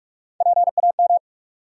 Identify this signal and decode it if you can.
Morse code, heard as a keyed tone, about 46 wpm, 700 Hz. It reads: PAM
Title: USA Common Names - Max 3 Letters - 45wpm